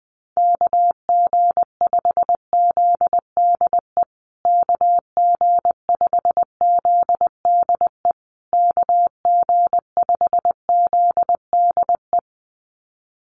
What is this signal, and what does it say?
Morse code, heard as a keyed tone, about 20 wpm, 700 Hz. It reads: KG5ZDE KG5ZDE KG5ZDE
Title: USA Call Signs Repeated 3x 20wpm